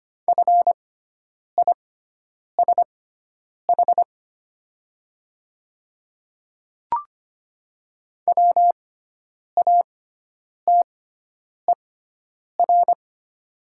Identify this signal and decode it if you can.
Morse code, heard as a keyed tone, about 25 wpm, spaced as for 9 wpm, 690 Hz. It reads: FISH WATER
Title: 6x Character Spacing - Top 300 Words - 25wpm